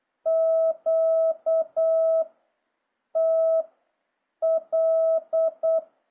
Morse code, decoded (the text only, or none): Q T L